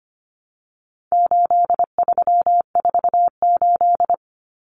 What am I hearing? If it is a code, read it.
Morse code, heard as a keyed tone, about 25 wpm, 700 Hz. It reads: 8348
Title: Common Cut Numbers - 4-Digit 25wpm